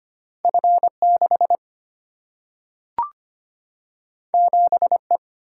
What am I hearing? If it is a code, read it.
Morse code, heard as a keyed tone, about 25 wpm, 700 Hz. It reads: F 6 7 E
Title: Single-Letter Single-Number 25wpm